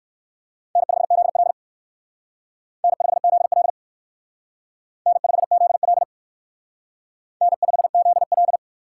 Morse code, heard as a keyed tone, about 59 wpm, 700 Hz. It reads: N5ZL N5ZL N5ZL N5ZL